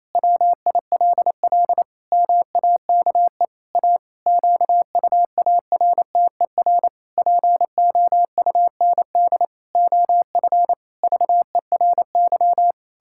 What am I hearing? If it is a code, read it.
Morse code, heard as a keyed tone, about 28 wpm, 700 Hz. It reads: WILL MAKE A QUARTER POUND OF VERY
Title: Pangram Challenge v2 - 28wpm